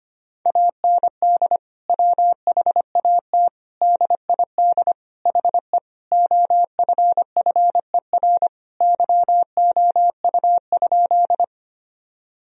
Morse code, heard as a keyed tone, about 25 wpm, 700 Hz. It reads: AND WHAT DID HE OFFER YOU?
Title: Sentences from Top 1000 Words - Review Part 4 - 25wpm